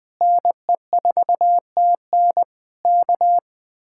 Morse code, heard as a keyed tone, about 20 wpm, 700 Hz. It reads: NE4TN K
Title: Speed-Racing Common QSO Phrases - 20wpm